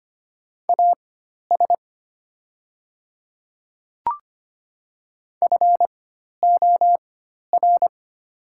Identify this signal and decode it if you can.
Morse code, heard as a keyed tone, about 25 wpm, spaced as for 12 wpm, 690 Hz. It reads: AS FOR
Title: 4x Character Spacing - Top 100 Words - 25wpm